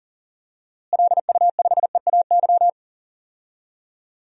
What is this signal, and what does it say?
Morse code, heard as a keyed tone, about 40 wpm, 700 Hz. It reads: LU5EAY
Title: DXCC-100 Call Signs - 6 Characters - Repeated 3x - 40wpm